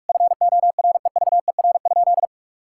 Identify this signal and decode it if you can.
Morse code, heard as a keyed tone, about 46 wpm, 700 Hz. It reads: FOREVER?
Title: Sentences from Top 1600 Words 45wpm